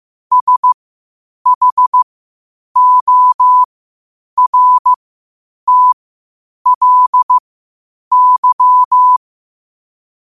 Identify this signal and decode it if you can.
Morse code, heard as a keyed tone, about 15 wpm, spaced as for 8 wpm, 1000 Hz. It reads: SHORTLY
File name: qotdfiles / 8wpm